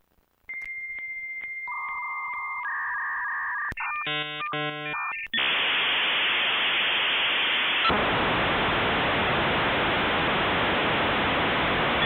Dial-up modem handshake.